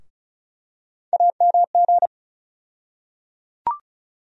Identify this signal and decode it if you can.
Morse code, heard as a keyed tone, about 36 wpm, 700 Hz. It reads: AMG
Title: USA Call Sign Postfixes - 3 Letters 35wpm